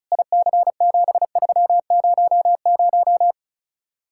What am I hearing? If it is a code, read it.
Morse code, heard as a keyed tone, about 45 wpm, 700 Hz. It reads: IC7300